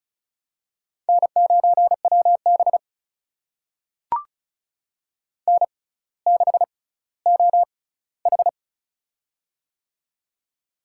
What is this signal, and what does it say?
Morse code, heard as a keyed tone, about 36 wpm, spaced as for 18 wpm, 700 Hz. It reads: N9WB N6OH
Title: USA Call Signs - 4 Characters - 6x Character Spacing - 35wpm